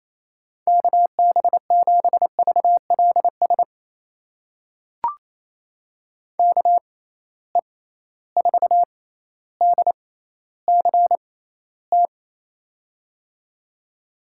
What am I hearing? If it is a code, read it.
Morse code, heard as a keyed tone, about 28 wpm, spaced as for 14 wpm, 700 Hz. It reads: KB7VLS KE4DCT
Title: USA Call Signs - 6 Characters - 6x Character Spacing - 28wpm